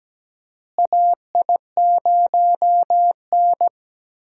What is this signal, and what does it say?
Morse code, heard as a keyed tone, about 17 wpm, 700 Hz. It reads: AI0N
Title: USA Call Signs - 4 Characters - Repeated 2x - 17wpm